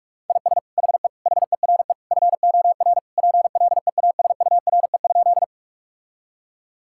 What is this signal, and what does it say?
Morse code, heard as a keyed tone, about 46 wpm, 700 Hz. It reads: IS HE HERE FOR PLEASURE?
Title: Sentences from Top 1600 Words 45wpm